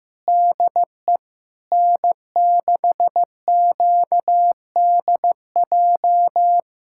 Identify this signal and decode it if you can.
Morse code, heard as a keyed tone, about 15 wpm, 700 Hz. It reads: DE N6QDJ
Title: USA Callsign DE Callsign 15wpm